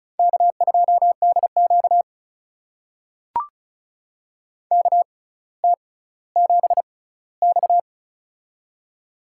Morse code, heard as a keyed tone, about 34 wpm, spaced as for 18 wpm, 700 Hz. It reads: K2DQ KT7X
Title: USA Call Signs - 4 Characters - 6x Character Spacing - 35wpm